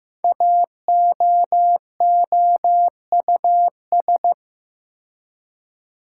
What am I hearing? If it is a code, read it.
Morse code, heard as a keyed tone, about 15 wpm, 700 Hz. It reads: AOOUS
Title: Code Groups - 5 Letters - 15wpm